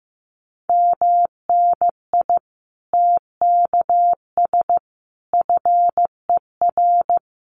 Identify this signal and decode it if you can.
Morse code, heard as a keyed tone, about 15 wpm, 700 Hz. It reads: MNI TKS FER
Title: Common QSO Phrases 15wpm